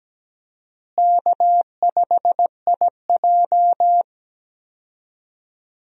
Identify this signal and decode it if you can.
Morse code, heard as a keyed tone, about 17 wpm, 700 Hz. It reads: K5IJ
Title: USA Call Signs - 4 Characters - Repeated 2x - 17wpm